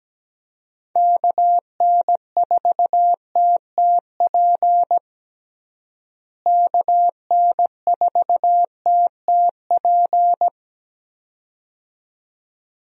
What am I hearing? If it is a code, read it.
Morse code, heard as a keyed tone, about 17 wpm, 700 Hz. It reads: KN4TTP KN4TTP
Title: USA Call Signs - 6 Characters - Repeated 2x - 17wpm